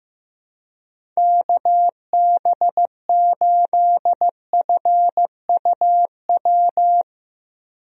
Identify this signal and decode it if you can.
Morse code, heard as a keyed tone, about 15 wpm, 700 Hz. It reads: KB8FUW
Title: USA Call Signs - 6 Characters - Repeated 3x - 15wpm